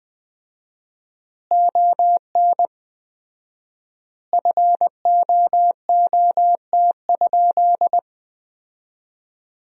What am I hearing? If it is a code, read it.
Morse code, heard as a keyed tone, about 20 wpm, 700 Hz. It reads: ON FOOT?
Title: 4x Word-Spacing - Sentences from Top 500 Words - 20wpm